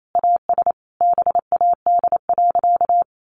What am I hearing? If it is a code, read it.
Morse code, heard as a keyed tone, about 28 wpm, 700 Hz. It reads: AS BAD.
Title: Sentences from Top 1200 Words 28wpm